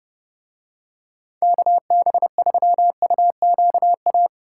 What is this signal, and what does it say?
Morse code, heard as a keyed tone, about 29 wpm, 700 Hz. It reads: KB3UQA